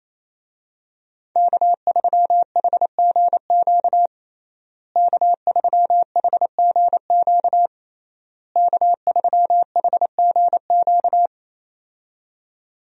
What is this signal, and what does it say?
Morse code, heard as a keyed tone, about 28 wpm, 700 Hz. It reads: K3HGQ K3HGQ K3HGQ
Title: USA Call Signs - 5 Characters - Repeated 3x - 28wpm